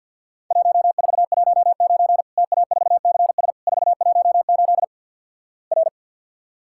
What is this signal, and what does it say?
Morse code, heard as a keyed tone, about 50 wpm, 700 Hz. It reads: 1419 TA4CS 418 R
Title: Worked All Europe DX Contest - QTC Exchange - 50wpm